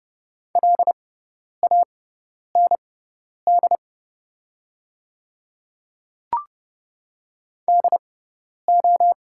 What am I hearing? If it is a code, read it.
Morse code, heard as a keyed tone, about 30 wpm, spaced as for 10 wpm, 690 Hz. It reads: LAND DO
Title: 6x Character Spacing - Top 200 Words - 30wpm